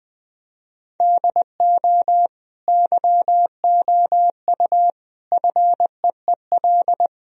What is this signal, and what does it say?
Morse code, heard as a keyed tone, about 20 wpm, 700 Hz. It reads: DO YOU FEEL
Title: Sentences from Top 1000 Words - Review Part 3 - 20wpm